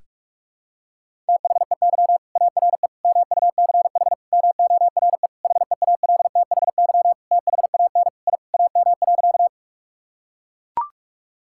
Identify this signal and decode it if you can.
Morse code, heard as a keyed tone, about 43 wpm, 700 Hz. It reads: THEY ARE MUCH MORE HEALTHY THAN I AM.
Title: Sentences from Top 1000 Words - Review Part 5 - 45wpm